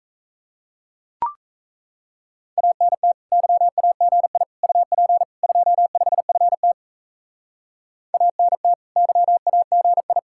Morse code, heard as a keyed tone, about 37 wpm, 700 Hz. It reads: ANT YAGI UP 25FT ANT YAGI